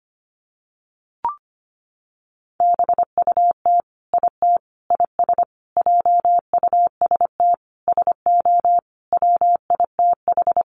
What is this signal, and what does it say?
Morse code, heard as a keyed tone, about 25 wpm, 700 Hz. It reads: BUT IT IS JUST SO WITH
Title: Sentences from Top 200 Words 25wpm